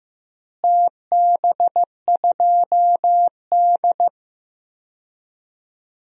Morse code, heard as a keyed tone, about 15 wpm, 700 Hz. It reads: TB2D
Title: DXCC-100 Call Signs - 4 Characters - Repeated 3x - 15wpm